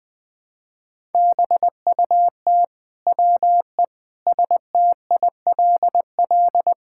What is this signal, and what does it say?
Morse code, heard as a keyed tone, about 20 wpm, 700 Hz. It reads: BUT WE STILL
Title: Sentences with 9-Word Phrases - 20wpm